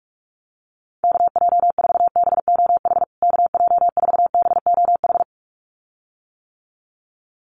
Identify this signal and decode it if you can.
Morse code, heard as a keyed tone, about 44 wpm, 700 Hz. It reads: KJ4BOH KJ4BOH
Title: USA Call Signs Repeated 2x 45wpm